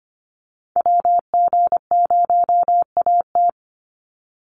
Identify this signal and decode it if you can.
Morse code, heard as a keyed tone, about 25 wpm, 700 Hz. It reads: WG0AT
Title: NA SOTA Chaser Callsigns 25wpm